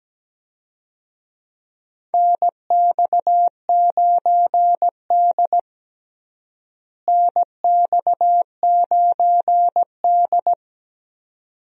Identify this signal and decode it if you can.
Morse code, heard as a keyed tone, about 17 wpm, 700 Hz. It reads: NX9D NX9D
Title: USA Call Signs - 4 Characters - Repeated 2x - 17wpm